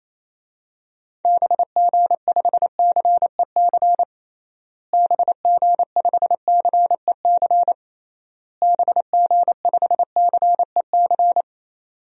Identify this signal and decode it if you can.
Morse code, heard as a keyed tone, about 28 wpm, 700 Hz. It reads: BG5CEC BG5CEC BG5CEC
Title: DXCC-100 Call Signs - 6 Characters - Repeated 3x - 28wpm